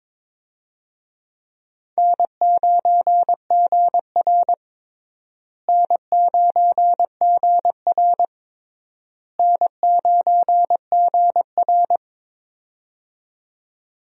Morse code, heard as a keyed tone, about 22 wpm, 700 Hz. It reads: N9GR N9GR N9GR
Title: USA Call Signs - 4 Characters - Repeated 3x - 22wpm